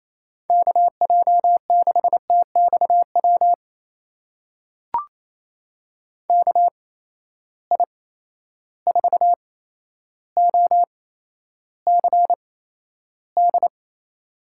Morse code, heard as a keyed tone, about 28 wpm, spaced as for 12 wpm, 700 Hz. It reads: KJ6TXW KI4OCD
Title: USA Call Signs - 6 Characters - 8x Character Spacing - 28wpm